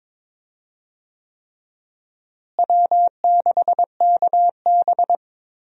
Morse code, heard as a keyed tone, about 21 wpm, 700 Hz. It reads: W6KB
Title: USA Call Signs - 4 Characters - Repeated 3x - 22wpm